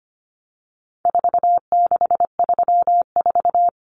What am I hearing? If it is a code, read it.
Morse code, heard as a keyed tone, about 25 wpm, 700 Hz. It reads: 4634
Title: Numbers - 4-Digit 25wpm